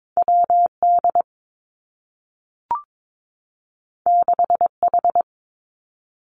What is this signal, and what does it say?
Morse code, heard as a keyed tone, about 22 wpm, 700 Hz. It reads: W D 6 H
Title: Two Challenging Letter-Number Combinations 22wpm